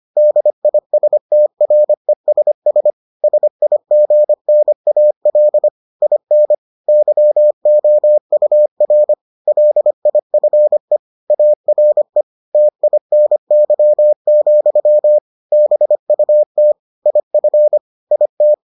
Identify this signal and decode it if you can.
Morse code, heard as a keyed tone, about 25 wpm, 600 Hz. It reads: DISTRESS SIGNAL IN YOUR LIFE ARE TINY, BUT IF IT